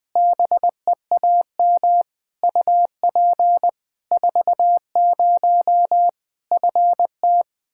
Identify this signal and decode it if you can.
Morse code, heard as a keyed tone, about 20 wpm, 700 Hz. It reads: BEAM UP 40 FT
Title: Practice QSO - Exchange 2 - Info - 20wpm